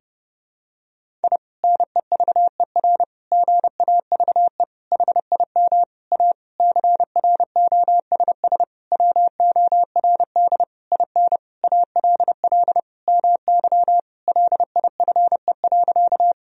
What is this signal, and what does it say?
Morse code, heard as a keyed tone, about 30 wpm, 700 Hz. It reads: I NEVER GAVE HIM A CROSS WORD IN ALL MY LIFE.
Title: Sentences from Top 1000 Words - Review Part 1 - 30wpm